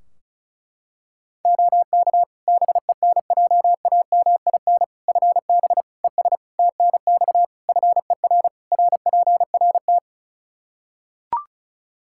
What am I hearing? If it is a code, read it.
Morse code, heard as a keyed tone, about 34 wpm, 700 Hz. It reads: OK BENJAMIN FB ES TNX FER RPRT
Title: K1USN SST QSO Practice 35wpm